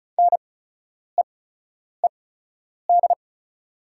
Morse code, heard as a keyed tone, about 34 wpm, spaced as for 10 wpm, 690 Hz. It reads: NEED
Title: 8x Character Spacing - Top 200 Words - 35wpm